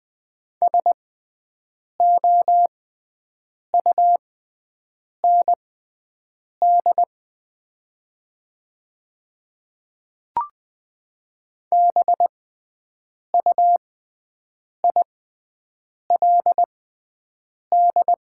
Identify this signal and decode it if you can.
Morse code, heard as a keyed tone, about 20 wpm, spaced as for 7 wpm, 700 Hz. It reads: SOUND BUILD
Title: 6x Character Spacing - Top 300 Words - 20wpm